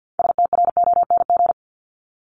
Morse code, heard as a keyed tone, about 48 wpm, 700 Hz. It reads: STRONG
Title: LY Suffix - Top 100 Words with Base Words 50wpm